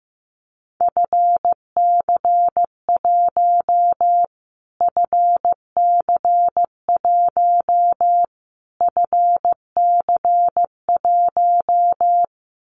Morse code, heard as a keyed tone, about 15 wpm, 700 Hz. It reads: FC1 FC1 FC1
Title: Callsign Prefixes for Top 25 DXCC - Repeated 3x - 15wpm